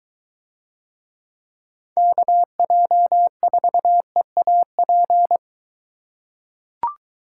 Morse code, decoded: KJ4EAP

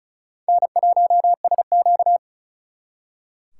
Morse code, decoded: N1SQ

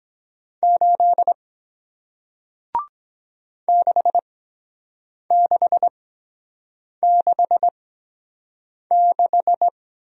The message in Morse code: 8 6666

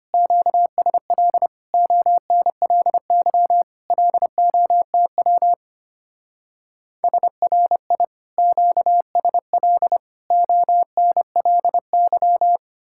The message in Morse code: QSL ONLY LOTW SRI QSL ONLY

